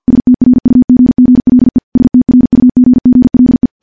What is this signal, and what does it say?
Morse code, heard as a keyed tone, about 50 wpm, 250 Hz. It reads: STRUGGLE STRUGGLE
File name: morse / morse-training-50WPM